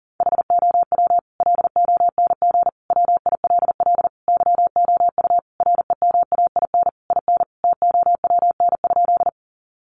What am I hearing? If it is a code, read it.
Morse code, heard as a keyed tone, about 40 wpm, 700 Hz. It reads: HOW LONG WILL YOU REMAIN IN TOWN?
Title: Sentences from Top 1200 Words 40wpm